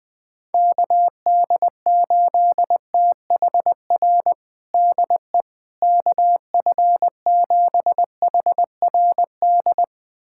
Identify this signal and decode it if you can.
Morse code, heard as a keyed tone, about 21 wpm, 700 Hz. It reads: KD8THR DE KF7HRD
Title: USA Callsign DE Callsign - 6 Characters - 20wpm